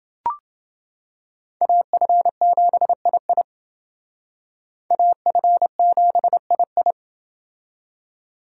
Morse code, heard as a keyed tone, about 28 wpm, 700 Hz. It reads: AF7II AF7II